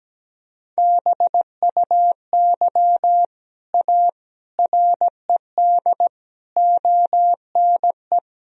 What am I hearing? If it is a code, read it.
Morse code, heard as a keyed tone, about 17 wpm, 700 Hz. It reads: BUY A RED ONE
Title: Sentences from Top 1000 Words - Review Part 2 - 17wpm